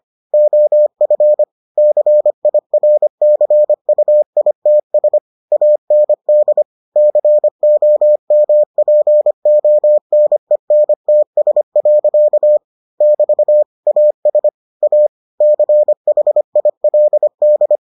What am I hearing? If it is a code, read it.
Morse code, heard as a keyed tone, about 25 wpm, 600 Hz. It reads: OF CIRCUITS AND COMPONENTS. <BT> AS A CHILD